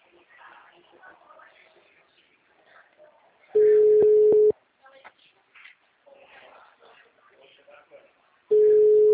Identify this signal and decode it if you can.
Ringback tone (Europe, 425 Hz), with speech.